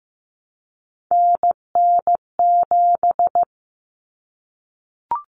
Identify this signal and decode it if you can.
Morse code, heard as a keyed tone, about 15 wpm, 700 Hz. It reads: NN7
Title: Common Cut Numbers - 3-Digit 15wpm